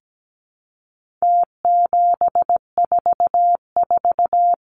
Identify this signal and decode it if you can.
Morse code, heard as a keyed tone, about 17 wpm, 700 Hz. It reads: T744
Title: Numbers - Contest Style - 4-Digit 17wpm